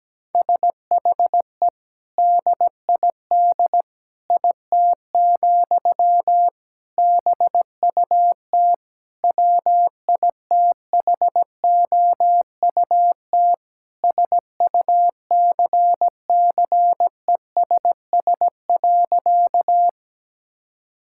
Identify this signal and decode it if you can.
Morse code, heard as a keyed tone, about 17 wpm, 700 Hz. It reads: SHE DID IT, BUT WITHOUT SUCCESS.
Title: Sentences from Top 1000 Words - Review Part 6 - 17wpm